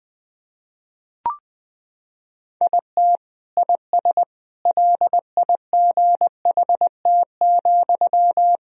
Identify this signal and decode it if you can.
Morse code, heard as a keyed tone, about 20 wpm, 700 Hz. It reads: IT IS LIGHT,
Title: Sentences from Top 1000 Words - Review Part 1 - 20wpm